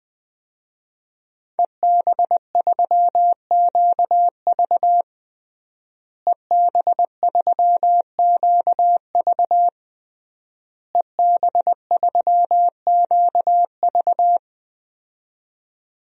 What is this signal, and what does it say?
Morse code, heard as a keyed tone, about 20 wpm, 700 Hz. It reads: EB3QV EB3QV EB3QV
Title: DXCC-100 Call Signs - 5 Characters - Repeated 3x - 20wpm